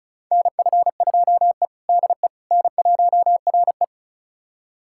Morse code, heard as a keyed tone, about 36 wpm, 700 Hz. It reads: NF2E DE N1RE
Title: USA Callsign DE Callsign - 4 Characters - 35wpm